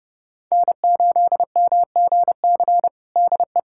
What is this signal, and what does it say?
Morse code, heard as a keyed tone, about 32 wpm, 700 Hz. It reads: N8MGC DE
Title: USA Callsign DE Callsign 30wpm